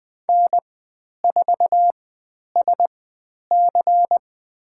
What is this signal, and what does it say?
Morse code, heard as a keyed tone, about 20 wpm, spaced as for 10 wpm, 700 Hz. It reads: N4SC
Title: 102 Lesson 17g - Letter u - Call Signs 20wpm